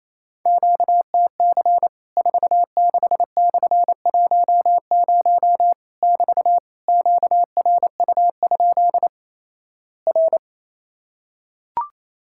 Morse code, heard as a keyed tone, about 28 wpm, 700 Hz. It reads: QTC 46/10 <BT> QRU? R
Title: Worked All Europe DX Contest - QTC Exchange - 28wpm